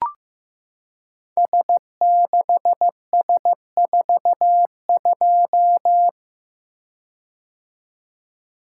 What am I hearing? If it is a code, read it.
Morse code, heard as a keyed tone, about 15 wpm, 700 Hz. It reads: S6S42
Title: Code Groups - 5 Letter:Numbers - 15wpm